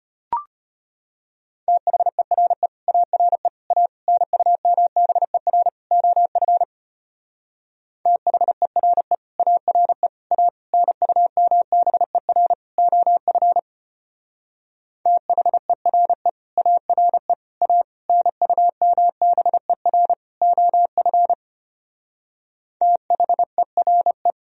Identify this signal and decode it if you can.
Morse code, heard as a keyed tone, about 31 wpm, 700 Hz. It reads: THERE ARE A NUMBER OF THERE ARE A NUMBER OF THERE ARE A NUMBER OF THERE
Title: Speed-Racing - Sets of 5 Words 25wpm